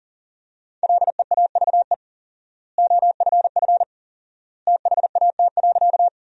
Morse code, heard as a keyed tone, about 40 wpm, 700 Hz. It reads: LEAVE OFF THAT.
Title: 4x Word-Spacing - Sentences from Top 400 Words - 40wpm